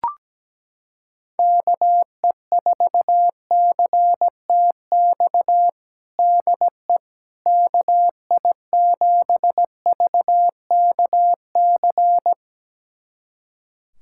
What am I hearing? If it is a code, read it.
Morse code, heard as a keyed tone, about 17 wpm, 700 Hz. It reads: KE4CTX DE KI7VKC
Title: USA Callsign DE Callsign - 6 Characters - 17wpm